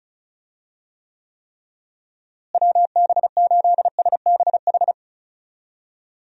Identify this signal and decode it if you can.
Morse code, heard as a keyed tone, about 34 wpm, 700 Hz. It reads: WB8SBH
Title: USA Call Signs - 6 Characters - Repeated 3x - 35wpm